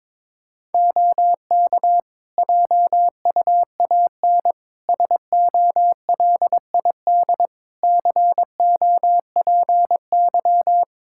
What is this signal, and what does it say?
Morse code, heard as a keyed tone, about 22 wpm, 700 Hz. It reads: OK JUAN SOLID COPY
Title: K1USN SST QSO Practice 22wpm